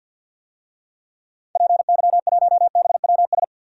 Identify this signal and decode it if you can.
Morse code, heard as a keyed tone, about 51 wpm, 700 Hz. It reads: PY1BWS
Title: DXCC-100 Call Signs - 6 Characters - 50wpm